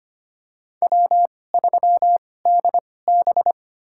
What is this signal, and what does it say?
Morse code, heard as a keyed tone, about 25 wpm, spaced as for 18 wpm, 700 Hz. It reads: W3DB
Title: USA Call Signs - 4 Characters - 2x Character Spacing - 25wpm